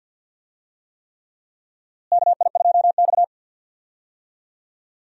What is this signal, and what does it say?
Morse code, heard as a keyed tone, about 47 wpm, 700 Hz. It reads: KI2X
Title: USA Call Signs - 4 Characters - Repeated 3x - 50wpm